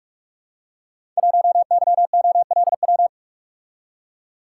Morse code, heard as a keyed tone, about 46 wpm, 700 Hz. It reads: 1YOLW
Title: Code Groups - 5 Letter:Numbers - 45wpm